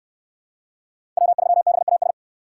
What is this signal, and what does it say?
Morse code, heard as a keyed tone, about 66 wpm, 700 Hz. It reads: W3BAS